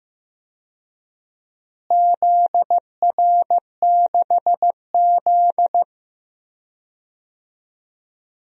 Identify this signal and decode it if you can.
Morse code, heard as a keyed tone, about 16 wpm, 700 Hz. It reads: ZR6Z
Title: DXCC-100 Call Signs - 4 Characters - Repeated 2x - 15wpm